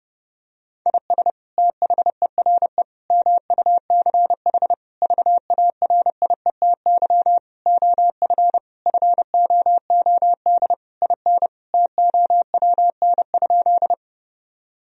Morse code, heard as a keyed tone, about 32 wpm, 700 Hz. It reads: IS THERE MUCH VARIETY OF FOOD IN TOWN?
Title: Sentences from Top 1000 Words - Review Part 4 - 30wpm